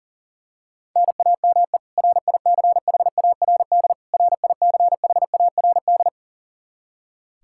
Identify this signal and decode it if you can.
Morse code, heard as a keyed tone, about 40 wpm, 700 Hz. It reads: NAME RICHARD RICHARD
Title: Practice QSO - Exchange 1 - Beginning - 40wpm